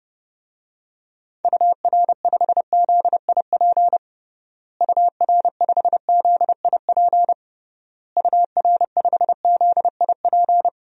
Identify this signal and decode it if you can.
Morse code, heard as a keyed tone, about 32 wpm, 700 Hz. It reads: UR5ZIP UR5ZIP UR5ZIP
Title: DXCC-100 Call Signs - 6 Characters - Repeated 3x - 30wpm